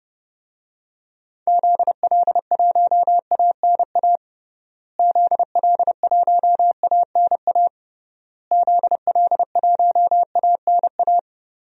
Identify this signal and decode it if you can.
Morse code, heard as a keyed tone, about 30 wpm, 700 Hz. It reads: ZL1ANA ZL1ANA ZL1ANA
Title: DXCC-100 Call Signs - 6 Characters - Repeated 3x - 30wpm